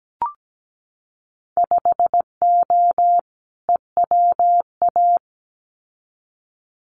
Morse code, heard as a keyed tone, about 17 wpm, 700 Hz. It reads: 5O EWA
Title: Winter Field Day Exchanges Encore 17wpm